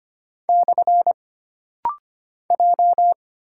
Morse code, heard as a keyed tone, about 25 wpm, 700 Hz. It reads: / J
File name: Single Character (Letter:Number:Punctuation:Prosign) - Rapid-Fire - 25wpm